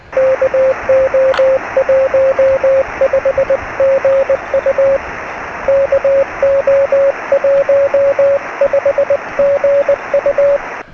Morse code, heard as a keyed tone, about 20 wpm, 530 Hz, with radio noise. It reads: KO15GU KO15GU
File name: sm7lcb / 0426 ly2baw 13cm